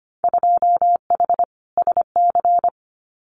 Morse code, heard as a keyed tone, about 25 wpm, 690 Hz. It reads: 2H SC
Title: Winter Field Day Exchanges 25wpm